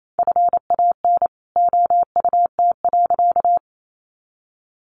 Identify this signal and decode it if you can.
Morse code, heard as a keyed tone, about 28 wpm, 700 Hz. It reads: FAN OUT.
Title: Sentences from Top 1300 Words 28wpm